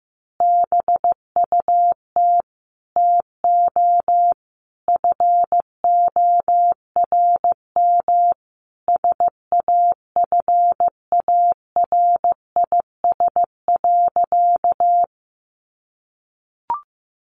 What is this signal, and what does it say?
Morse code, heard as a keyed tone, about 15 wpm, 700 Hz. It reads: BUT TO FORM SAFARIS.
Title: A Sound Of Thunder 15wpm